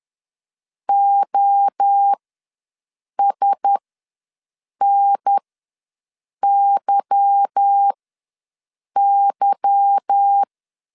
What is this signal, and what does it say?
Morse code, heard as a keyed tone, about 10 wpm, spaced as for 6 wpm, 800 Hz. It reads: OSNYY